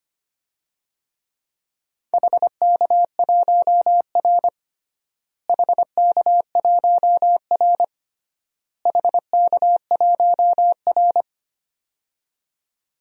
Morse code, heard as a keyed tone, about 25 wpm, 700 Hz. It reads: HK1R HK1R HK1R
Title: DXCC-100 Call Signs - 4 Characters - Repeated 3x - 25wpm